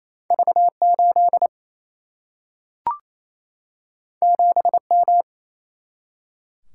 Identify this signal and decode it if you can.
Morse code, heard as a keyed tone, about 28 wpm, 700 Hz. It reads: V 8 7 M
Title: Single-Letter Single-Number 28wpm